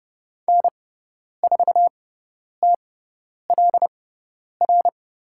Morse code, heard as a keyed tone, about 30 wpm, spaced as for 10 wpm, 700 Hz. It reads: N4TLR
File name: Lesson 18g - Letter m - Call Signs 30wpm